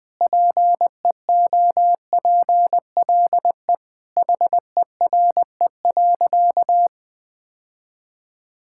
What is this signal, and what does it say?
Morse code, heard as a keyed tone, about 20 wpm, 700 Hz. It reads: PEOPLE HERE.
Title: Sentences with 6-Word Phrases - 20wpm